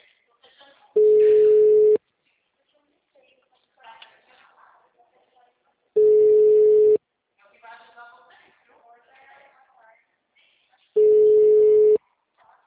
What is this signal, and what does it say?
Ringback tone (Europe, 425 Hz), with speech.